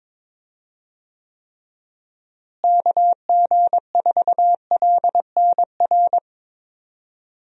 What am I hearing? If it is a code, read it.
Morse code, heard as a keyed tone, about 22 wpm, 700 Hz. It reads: KG4LNR
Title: USA Call Signs - 6 Characters - Repeated 3x - 22wpm